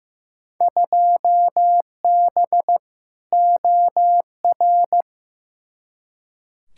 Morse code, heard as a keyed tone, about 15 wpm, 700 Hz. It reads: 2B OR
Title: Field Day Exchanges Encore - 15wpm